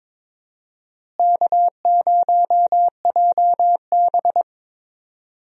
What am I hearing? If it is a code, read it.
Morse code, heard as a keyed tone, about 22 wpm, 700 Hz. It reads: K0JB
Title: USA Call Signs - 4 Characters - Repeated 3x - 22wpm